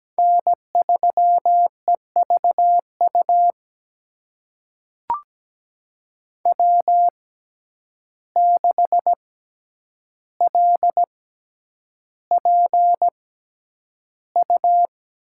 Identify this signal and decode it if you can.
Morse code, heard as a keyed tone, about 17 wpm, spaced as for 9 wpm, 700 Hz. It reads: N3EVU W6LPU
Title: USA Call Signs - 5 Characters - 6x Character Spacing - 17wpm